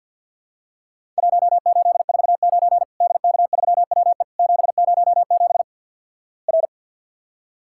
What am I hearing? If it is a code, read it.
Morse code, heard as a keyed tone, about 49 wpm, 700 Hz. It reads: 1849 DK3WE 707 R
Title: Worked All Europe DX Contest - QTC Exchange with Repeats - 50wpm